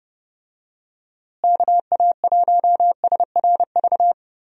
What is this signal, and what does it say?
Morse code, heard as a keyed tone, about 30 wpm, 700 Hz. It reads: KA1SRV
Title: USA Call Signs - 6 Characters - Repeated 3x - 30wpm